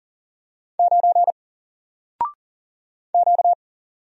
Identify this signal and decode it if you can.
Morse code, heard as a keyed tone, about 40 wpm, 700 Hz. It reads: 9 Q
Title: Single Letter-Number 40wpm